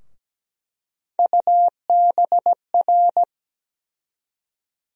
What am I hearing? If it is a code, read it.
Morse code, heard as a keyed tone, about 17 wpm, 700 Hz. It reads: UBR
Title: USA Call Sign Postfixes - 3 Letters 17wpm